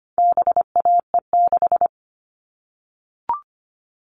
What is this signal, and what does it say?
Morse code, heard as a keyed tone, about 25 wpm, 700 Hz. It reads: BAE6
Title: All Cut Numbers - 4-Digit 25wpm